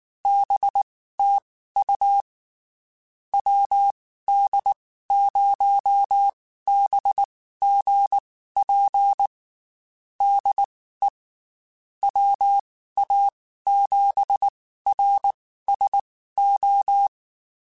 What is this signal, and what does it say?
Morse code, heard as a keyed tone, about 19 wpm, spaced as for 14 wpm, 800 Hz. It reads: BTU WD0BGP DE WA7RSO
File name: QSOs / QSOs-mp3s / WD0BGP 1@18